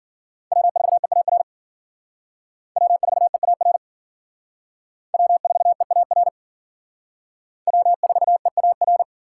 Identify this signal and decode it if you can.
Morse code, heard as a keyed tone, about 52 wpm, 700 Hz. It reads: W4EAR W4EAR W4EAR W4EAR